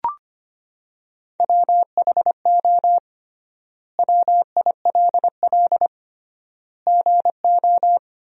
Morse code, heard as a keyed tone, about 25 wpm, 700 Hz. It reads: WHO WILL GO
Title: 3x Word-Spacing - Sentences from Top 100 Words - 25wpm